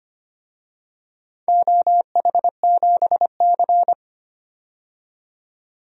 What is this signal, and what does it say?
Morse code, heard as a keyed tone, about 25 wpm, 700 Hz. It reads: OH7C
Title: DXCC-100 Call Signs - 4 Characters - 25wpm